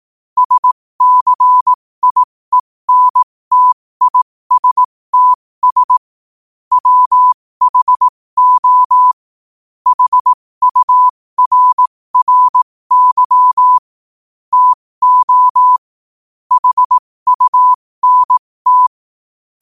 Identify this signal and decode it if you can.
Morse code, heard as a keyed tone, about 18 wpm, spaced as for 15 wpm, 1000 Hz. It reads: SCIENTISTS WHO HURRY TO HUNT
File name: qotdfiles / 15wpm